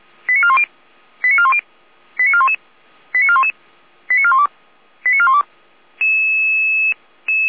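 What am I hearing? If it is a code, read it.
Five-tone selective call (ZVEI), heard as quick runs of five tones, tones of about 70 ms. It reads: 89410 89410 89411 89411 89412 89412
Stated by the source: ZVEI five-tone selective call 88169, received by SDR, recorded by Feuerwehr Orlamünde (fire brigade).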